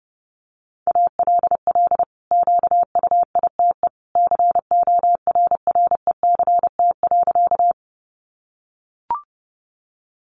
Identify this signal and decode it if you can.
Morse code, heard as a keyed tone, about 30 wpm, 700 Hz. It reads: ALL QUITE CORRECT.
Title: Sentences from Top 700 Words 30wpm